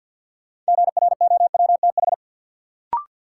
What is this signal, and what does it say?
Morse code, heard as a keyed tone, about 49 wpm, 700 Hz. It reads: GROWTH
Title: Sentences from Top 1000 Words - Review Part 5 - New Word Drill - 50wpm